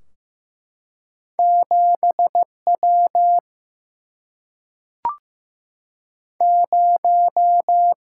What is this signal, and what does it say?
Morse code, heard as a keyed tone, about 15 wpm, 700 Hz. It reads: 7W 0
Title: Single-Letter Single-Number 15wpm